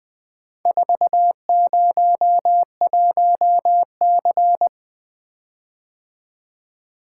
Morse code, heard as a keyed tone, about 20 wpm, 700 Hz. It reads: 401C
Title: Four Letter-Number Combinations - 20wpm